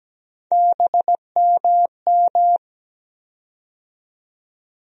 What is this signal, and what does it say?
Morse code, heard as a keyed tone, about 17 wpm, 700 Hz. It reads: BMM
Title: USA Call Sign Postfixes - 3 Letters 17wpm